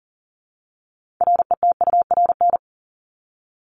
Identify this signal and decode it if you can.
Morse code, heard as a keyed tone, about 40 wpm, 700 Hz. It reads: RETURN
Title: ED Suffix - Top 100 Words with Base Words 40wpm